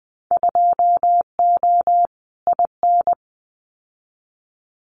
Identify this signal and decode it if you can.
Morse code, heard as a keyed tone, about 20 wpm, 700 Hz. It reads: 2O IN
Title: Winter Field Day Exchanges 20wpm